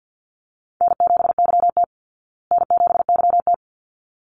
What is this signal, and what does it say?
Morse code, heard as a keyed tone, about 50 wpm, 700 Hz. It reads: N7YT N7YT
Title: Easy Call Signs Repeated 2x - 50wpm